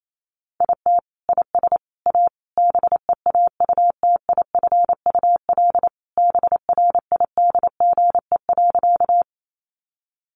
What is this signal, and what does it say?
Morse code, heard as a keyed tone, about 28 wpm, 700 Hz. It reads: IT IS A BEAUTIFUL BRIDGE.
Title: Sentences from Top 1200 Words 28wpm